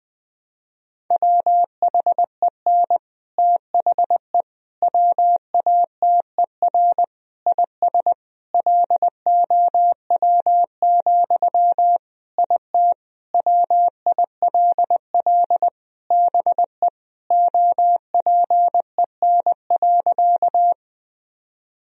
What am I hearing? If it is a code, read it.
Morse code, heard as a keyed tone, about 21 wpm, 700 Hz. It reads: WHEN THE WATER IS LOW, IT WILL BE OPEN.
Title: Sentences from Top 1000 Words - Review Part 1 - 20wpm